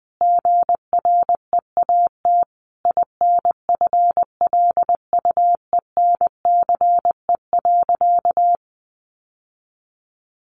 Morse code, heard as a keyed tone, about 20 wpm, 700 Hz. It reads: GREAT INFLUENCE.
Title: Sentences from Top 1100 Words 20wpm